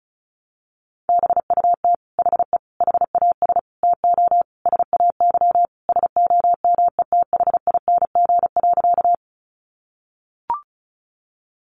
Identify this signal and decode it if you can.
Morse code, heard as a keyed tone, about 34 wpm, 700 Hz. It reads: BUT HE HAS TO SAY SOMETHING.
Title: Sentences from Top 600 Words 35wpm